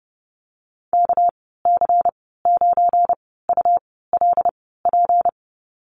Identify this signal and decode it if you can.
Morse code, heard as a keyed tone, about 30 wpm, spaced as for 17 wpm, 700 Hz. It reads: KC9ULP
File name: Lesson 37g - Letter x - Call Signs 30wpm